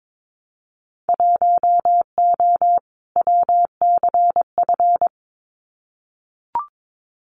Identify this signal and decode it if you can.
Morse code, heard as a keyed tone, about 22 wpm, 700 Hz. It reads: 1O WCF
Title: Winter Field Day Exchanges 22wpm